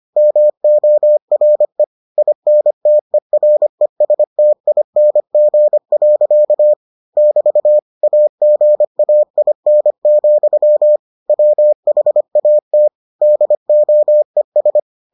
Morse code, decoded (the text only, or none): MORE INTERESTING. <BT> AGAIN, WHAT DOES